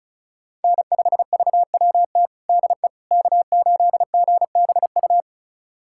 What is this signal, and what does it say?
Morse code, heard as a keyed tone, about 36 wpm, 700 Hz. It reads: N5VWT DE K8GBU
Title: USA Callsign DE Callsign - 5 Characters - 35wpm